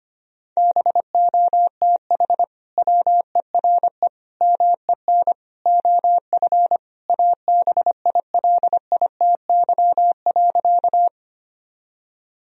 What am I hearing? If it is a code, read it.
Morse code, heard as a keyed tone, about 25 wpm, 700 Hz. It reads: BOTH WERE MEN OF ABILITY.
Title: Sentences from Top 1000 Words - Review Part 3 - 25wpm